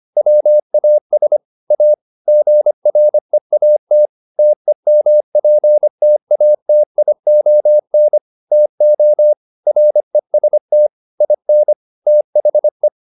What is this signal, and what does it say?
Morse code, heard as a keyed tone, about 25 wpm, 600 Hz. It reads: WAS A GREAT TEMPTATION TO REST IN THE